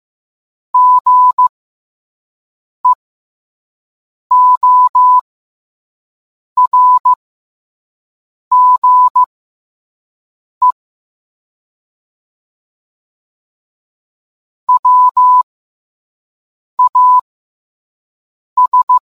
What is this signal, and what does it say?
Morse code, heard as a keyed tone, about 15 wpm, spaced as for 5 wpm, 1000 Hz. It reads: GEORGE WAS